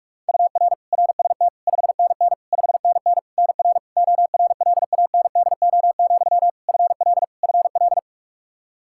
Morse code, heard as a keyed tone, about 46 wpm, 700 Hz. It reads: UR RST 5NN 5NN NR ORLANDO, FL FL
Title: Practice QSO - Exchange 1 - Beginning - 45wpm